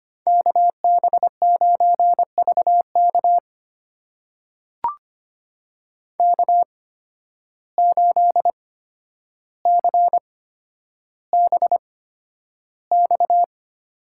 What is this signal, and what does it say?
Morse code, heard as a keyed tone, about 25 wpm, spaced as for 11 wpm, 700 Hz. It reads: KB9VK K8CBX
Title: USA Call Signs - 5 Characters - 8x Character Spacing - 25wpm